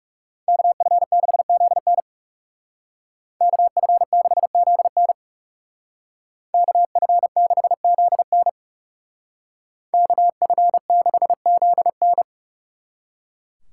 Morse code, heard as a keyed tone, about 37 wpm, 700 Hz. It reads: KF6ZN KF6ZN KF6ZN KF6ZN
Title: Speed Racing - USA Call Signs 30wpm